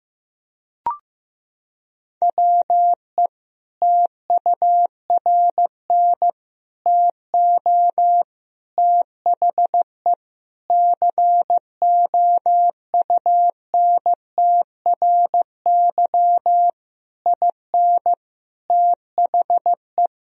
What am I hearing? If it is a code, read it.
Morse code, heard as a keyed tone, about 15 wpm, 700 Hz. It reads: WE TURN TO THE COUNTRY IN THE
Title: Sentences from Top 1000 Words - Review Part 1 - 15wpm